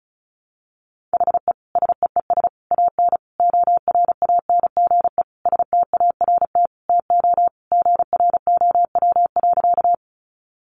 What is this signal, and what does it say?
Morse code, heard as a keyed tone, about 34 wpm, 700 Hz. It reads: HE SEES AN ORANGE START TO GROW.
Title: Easy Phrases 35wpm